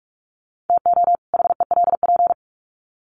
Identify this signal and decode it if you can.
Morse code, heard as a keyed tone, about 44 wpm, 700 Hz. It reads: TO HELP
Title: Sets of 2 Words 45wpm